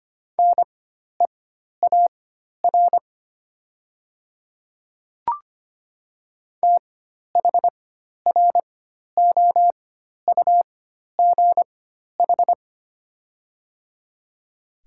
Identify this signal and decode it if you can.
Morse code, heard as a keyed tone, about 25 wpm, spaced as for 12 wpm, 700 Hz. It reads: NEAR THROUGH
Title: 4x Character Spacing - Top 300 Words - 25wpm